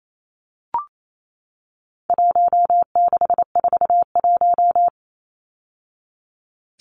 Morse code, heard as a keyed tone, about 28 wpm, 700 Hz. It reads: 1641
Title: Numbers - Contest Style - 4-Digit 28wpm